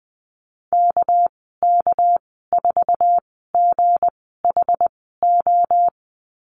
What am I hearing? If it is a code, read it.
Morse code, heard as a keyed tone, about 20 wpm, spaced as for 14 wpm, 700 Hz. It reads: KK4GHO
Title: 207 Lesson 33g - Letter j - Call Signs 20wpm